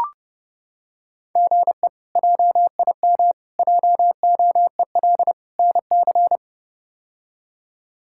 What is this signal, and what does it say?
Morse code, heard as a keyed tone, about 30 wpm, 700 Hz. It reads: GE JIM JOEL NC
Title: K1USN SST QSO Practice - 30wpm